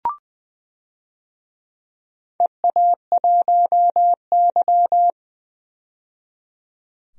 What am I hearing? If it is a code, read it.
Morse code, heard as a keyed tone, about 20 wpm, 700 Hz. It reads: EA1Y
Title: DXCC-100 Call Signs - 4 Characters - 20wpm